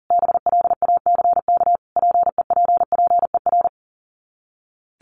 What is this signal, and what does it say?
Morse code, heard as a keyed tone, about 40 wpm, 700 Hz. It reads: BLACK PEPPER
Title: Sets of 5 Words Encore 40wpm